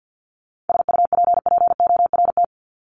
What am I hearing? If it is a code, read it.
Morse code, heard as a keyed tone, about 50 wpm, 700 Hz. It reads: SUPPORT
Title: Top 751-1000 Words 50wpm